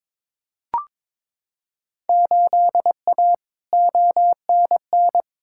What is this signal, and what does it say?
Morse code, heard as a keyed tone, about 22 wpm, 700 Hz. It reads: 8A ONN
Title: Field Day Exchanges Encore - 22wpm